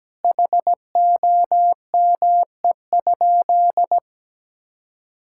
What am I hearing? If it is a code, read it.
Morse code, heard as a keyed tone, about 17 wpm, 700 Hz. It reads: HOME?
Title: Sentences from Top 1600 Words 17wpm